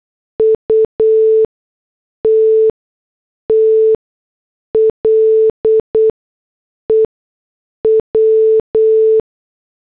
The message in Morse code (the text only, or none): UTTLE W